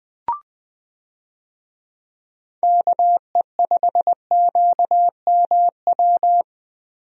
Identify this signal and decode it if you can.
Morse code, heard as a keyed tone, about 20 wpm, 700 Hz. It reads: KE5QMW
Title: USA Call Signs - 6 Characters - Repeated 3x - 20wpm